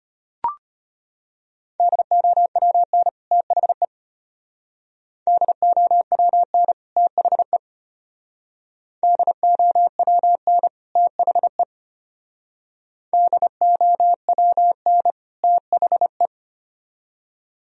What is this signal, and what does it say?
Morse code, heard as a keyed tone, about 30 wpm, 700 Hz. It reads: DOWN THE DOWN THE DOWN THE DOWN THE
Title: Speed-Racing - Sets of 2 Words 25wpm